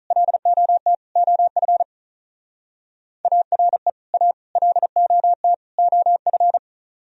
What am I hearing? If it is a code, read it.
Morse code, heard as a keyed tone, about 45 wpm, 700 Hz. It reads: LOT OF ARE A LOT OF